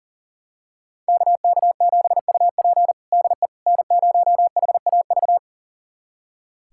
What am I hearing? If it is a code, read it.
Morse code, heard as a keyed tone, about 40 wpm, 700 Hz. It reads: KK7UP DE N0HAV
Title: USA Callsign DE Callsign - 5 Characters - 40wpm